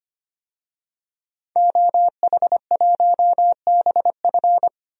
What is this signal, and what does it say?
Morse code, heard as a keyed tone, about 25 wpm, 700 Hz. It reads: OH1BF
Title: DXCC-100 Call Signs - 5 Characters - 25wpm